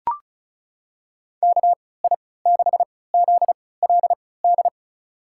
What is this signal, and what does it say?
Morse code, heard as a keyed tone, about 34 wpm, spaced as for 20 wpm, 690 Hz. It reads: KI6ZLD
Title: USA Call Signs - 3x Character Spacing - 35wpm